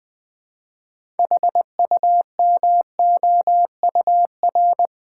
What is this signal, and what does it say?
Morse code, heard as a keyed tone, about 21 wpm, 700 Hz. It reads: HUMOUR
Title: 6 Letter Words - First 3 Letters Given - 20wpm